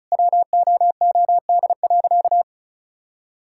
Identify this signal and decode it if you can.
Morse code, heard as a keyed tone, about 36 wpm, 700 Hz. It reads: WOOD.
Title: Sentences from Top 1000 Words - Review Part 6 - 35wpm